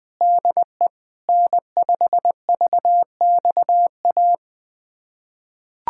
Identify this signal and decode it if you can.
Morse code, heard as a keyed tone, about 20 wpm, 700 Hz. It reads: DE N5VXA
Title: USA Callsign DE Callsign 20wpm